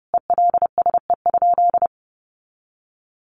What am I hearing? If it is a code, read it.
Morse code, heard as a keyed tone, about 30 wpm, 700 Hz. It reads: ELSE?
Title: Sentences from Top 900 Words 30wpm